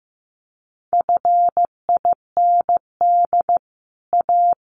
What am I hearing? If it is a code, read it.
Morse code, heard as a keyed tone, about 15 wpm, 700 Hz. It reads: FIND A
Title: Sets of 4 Words 15wpm